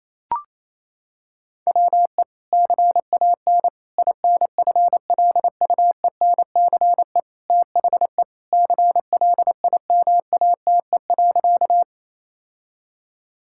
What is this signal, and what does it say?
Morse code, heard as a keyed tone, about 28 wpm, 700 Hz. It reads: WE CAN INFLUENCE THE CLIMATE.
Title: Sentences from Top 1600 Words 28wpm